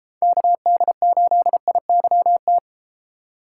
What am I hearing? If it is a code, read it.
Morse code, heard as a keyed tone, about 32 wpm, 700 Hz. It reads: KD8IYT